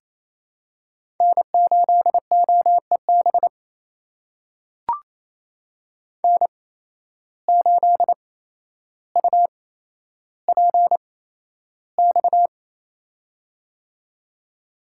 Morse code, heard as a keyed tone, about 28 wpm, spaced as for 12 wpm, 700 Hz. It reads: N8OEB N8UPX
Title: USA Call Signs - 5 Characters - 8x Character Spacing - 28wpm